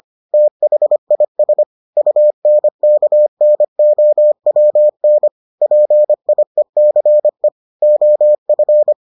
Morse code, heard as a keyed tone, about 25 wpm, 600 Hz. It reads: THIS UNKNOWN PIECE OF